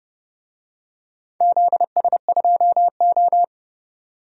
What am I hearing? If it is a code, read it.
Morse code, heard as a keyed tone, about 29 wpm, 700 Hz. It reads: ZS2O
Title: DXCC-100 Call Signs - 4 Characters - 30wpm